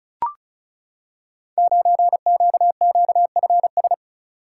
Morse code, heard as a keyed tone, about 36 wpm, 700 Hz. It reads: 9QQFS
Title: Code Groups - 5 Letter:Numbers - 35wpm